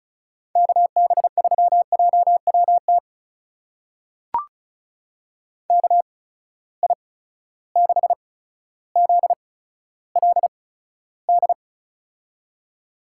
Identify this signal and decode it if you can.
Morse code, heard as a keyed tone, about 36 wpm, spaced as for 15 wpm, 700 Hz. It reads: KB3JWT KI6ZLD
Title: USA Call Signs - 6 Characters - 8x Character Spacing - 35wpm